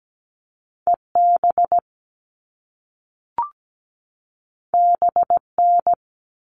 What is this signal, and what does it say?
Morse code, heard as a keyed tone, about 17 wpm, 700 Hz. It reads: EB BN
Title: All Cut Numbers - 2-Digit 17wpm